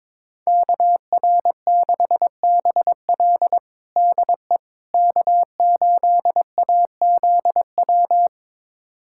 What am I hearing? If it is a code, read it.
Morse code, heard as a keyed tone, about 22 wpm, 700 Hz. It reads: KR6BL DE K8AZW
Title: USA Callsign DE Callsign - 5 Characters - 22wpm